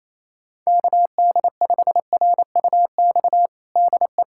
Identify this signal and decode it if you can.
Morse code, heard as a keyed tone, about 28 wpm, 700 Hz. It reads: KD5RUX DE
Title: USA Callsign DE Callsign - 6 Characters - 28wpm